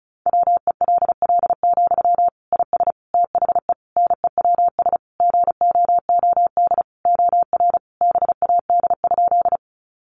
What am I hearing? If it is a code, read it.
Morse code, heard as a keyed tone, about 34 wpm, 700 Hz. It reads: WELL, IS THE NEWS GOOD OR BAD?
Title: Sentences from Top 800 Words 35wpm